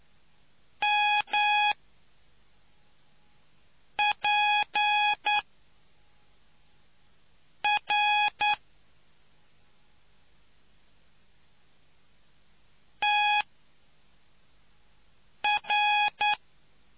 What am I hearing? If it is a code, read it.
Morse code, heard as a keyed tone, about 9 wpm, 830 Hz. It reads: M P R T R